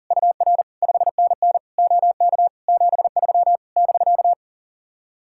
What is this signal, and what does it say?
Morse code, heard as a keyed tone, about 40 wpm, 700 Hz. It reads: UR 5NN OK 73 <BK>
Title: Speed-Racing Common QSO Phrases Encore - 40wpm